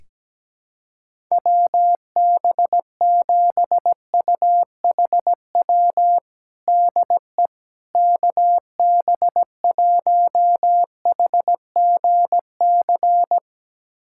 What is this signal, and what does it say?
Morse code, heard as a keyed tone, about 17 wpm, 700 Hz. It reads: WB7UHW DE KB1HGC
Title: USA Callsign DE Callsign 17wpm